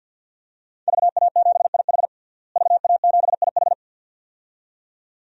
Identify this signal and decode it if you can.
Morse code, heard as a keyed tone, about 51 wpm, 700 Hz. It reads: VA7IH VA7IH
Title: DXCC-100 Call Signs - 5 Characters - Repeated 2x - 50wpm